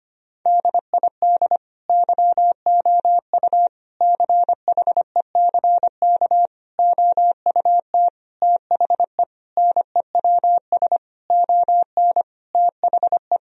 Morse code, heard as a keyed tone, about 25 wpm, 700 Hz. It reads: DID YOU CHECK OUT THE NEWS ON THE
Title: Sentences from Top 1000 Words - Review Part 4 - 25wpm